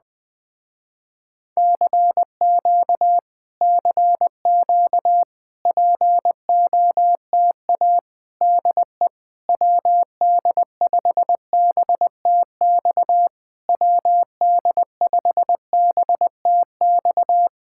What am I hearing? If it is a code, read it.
Morse code, heard as a keyed tone, about 19 wpm, 700 Hz. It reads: CQ CQ POTA DE WD5BTX WD5BTX
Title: Speed-Racing Common QSO Phrases - 20wpm